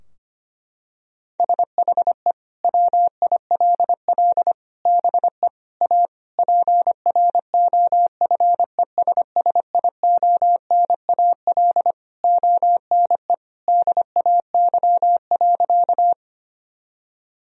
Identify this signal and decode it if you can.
Morse code, heard as a keyed tone, about 25 wpm, 700 Hz. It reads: SHE WILL BE A PROFESSIONAL ONE DAY.
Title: Sentences from Top 1000 Words - Review Part 4 - 25wpm